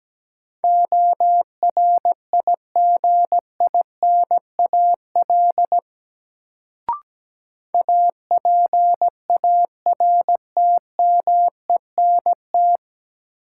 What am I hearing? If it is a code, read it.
Morse code, heard as a keyed tone, about 17 wpm, 700 Hz. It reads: ORIGINAL APARTMENT
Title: Sentences from Top 900 Words - New Word Drill - 17wpm